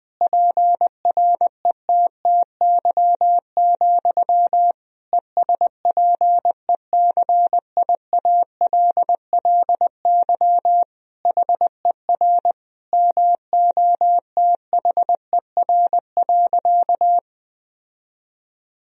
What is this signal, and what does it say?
Morse code, heard as a keyed tone, about 21 wpm, 700 Hz. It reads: PRETTY, ESPECIALLY HER MOTHER.
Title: Sentences from Top 1000 Words - Review Part 4 - 20wpm